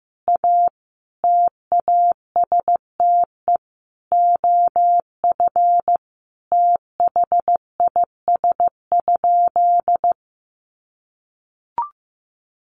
Morse code, heard as a keyed tone, about 15 wpm, 700 Hz. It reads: A TASTE OF THIS?
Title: Sentences from Top 1100 Words 15wpm